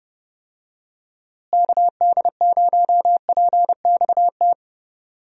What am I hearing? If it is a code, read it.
Morse code, heard as a keyed tone, about 30 wpm, 700 Hz. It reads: KD0PXT
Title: USA Call Signs - 6 Characters - Repeated 2x - 30wpm